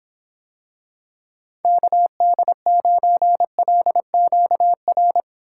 Morse code, heard as a keyed tone, about 26 wpm, 700 Hz. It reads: KD9LQR